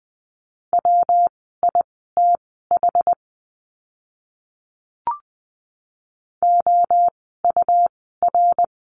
Morse code, heard as a keyed tone, about 20 wpm, spaced as for 14 wpm, 700 Hz. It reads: WITH OUR
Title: 160 Lesson 26f - Letter g - Words 20wpm